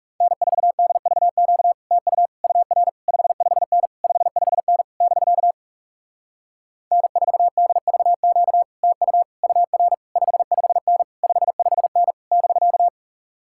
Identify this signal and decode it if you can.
Morse code, heard as a keyed tone, about 37 wpm, 700 Hz. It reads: N4DVQ TU UR 55N 55N <BK> N4DVQ TU UR 55N 55N <BK>
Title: Speed-Racing Common QSO Phrases - 30wpm